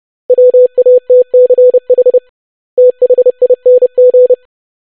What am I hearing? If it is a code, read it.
Morse code, heard as a keyed tone, about 30 wpm, 500 Hz. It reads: WATCH THING
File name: basic500-25wpm